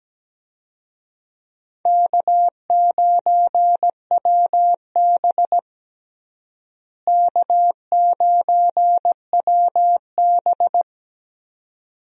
Morse code, heard as a keyed tone, about 17 wpm, 700 Hz. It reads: K9WB K9WB
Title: USA Call Signs - 4 Characters - Repeated 2x - 17wpm